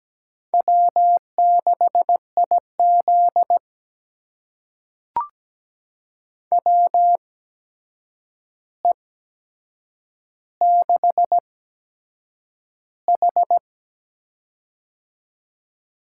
Morse code, heard as a keyed tone, about 17 wpm, spaced as for 7 wpm, 700 Hz. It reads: W6IZ WE6H
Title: USA Call Signs - 4 Characters - 8x Character Spacing - 17wpm